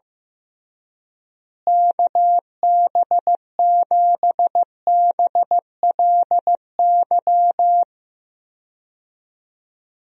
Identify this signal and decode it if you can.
Morse code, heard as a keyed tone, about 15 wpm, 700 Hz. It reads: KB7BLY
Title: USA Call Signs - 6 Characters - Repeated 3x - 15wpm